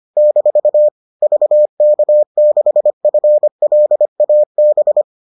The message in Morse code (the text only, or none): - VK6FLAB